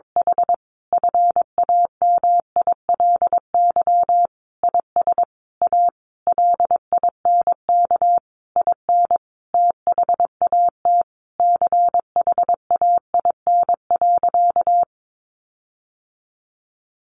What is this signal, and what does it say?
Morse code, heard as a keyed tone, about 22 wpm, 700 Hz. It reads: H FAMILY IS A LINK IN THAT CHAIN.
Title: Sentences from Top 1500 Words 22wpm